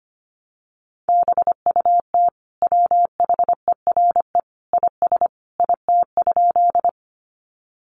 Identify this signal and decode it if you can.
Morse code, heard as a keyed tone, about 25 wpm, 700 Hz. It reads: BUT WHERE IS IT?
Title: Sentences from Top 200 Words 25wpm